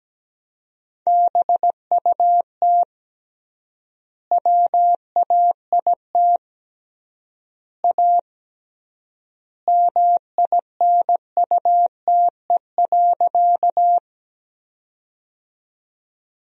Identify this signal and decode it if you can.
Morse code, heard as a keyed tone, about 17 wpm, 700 Hz. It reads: BUT WAIT A MINUTE.
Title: 3x Word-Spacing - Sentences from Top 500 Words - 17wpm